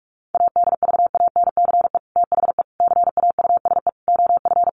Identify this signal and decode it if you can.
Morse code, heard as a keyed tone, about 44 wpm, 700 Hz. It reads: ADVANCE THE CAUSE OF
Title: Sentences from Top 1400 Words 45wpm